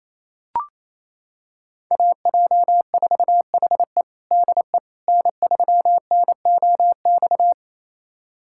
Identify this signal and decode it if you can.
Morse code, heard as a keyed tone, about 28 wpm, 700 Hz. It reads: AJ4HE DE N3NOX
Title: USA Callsign DE Callsign - 5 Characters - 28wpm